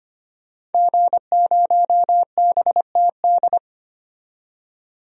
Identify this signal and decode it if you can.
Morse code, heard as a keyed tone, about 25 wpm, 700 Hz. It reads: G0BTD
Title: DXCC-100 Call Signs - 5 Characters - 25wpm